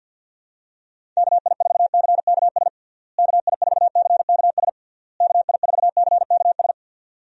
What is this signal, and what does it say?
Morse code, heard as a keyed tone, about 52 wpm, 700 Hz. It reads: KI4CKS KI4CKS KI4CKS
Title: USA Call Signs - 6 Characters - Repeated 3x - 50wpm